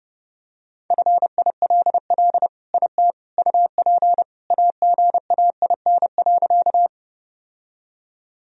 Morse code, heard as a keyed tone, about 32 wpm, 700 Hz. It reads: FILL IT UP AGAIN.
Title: Sentences from Top 1000 Words - Review Part 3 - 30wpm